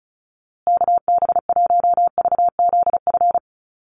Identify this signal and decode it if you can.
Morse code, heard as a keyed tone, about 34 wpm, 700 Hz. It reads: KB1VZF
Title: USA Call Signs Repeated 2x 35wpm